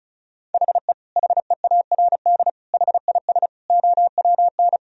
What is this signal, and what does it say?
Morse code, heard as a keyed tone, about 36 wpm, 700 Hz. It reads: HE HEARD HIS OWN
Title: Sentences from Top 1000 Words - Review Part 2 - 35wpm